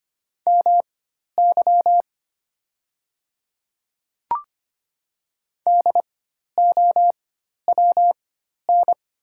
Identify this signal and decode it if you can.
Morse code, heard as a keyed tone, about 25 wpm, spaced as for 12 wpm, 690 Hz. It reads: MY DOWN
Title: 4x Character Spacing - Top 200 Words - 25wpm